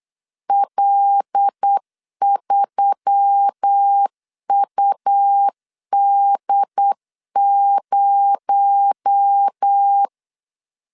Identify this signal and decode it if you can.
Morse code, heard as a keyed tone, about 8 wpm, 800 Hz. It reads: L3UD0
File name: LEZIONI / 6 PKX0 / 6 PKX0 8wpm 3341